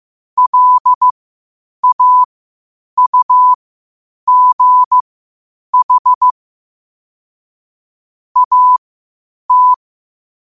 Morse code, heard as a keyed tone, about 15 wpm, spaced as for 8 wpm, 1000 Hz. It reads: LAUGH AT
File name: qotdfiles / 8wpm